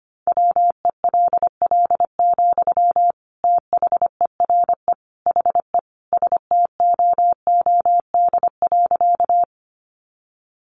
Morse code, heard as a keyed tone, about 25 wpm, 700 Hz. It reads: WELL, THERE HE STOOD.
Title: Sentences from Top 500 Words 25wpm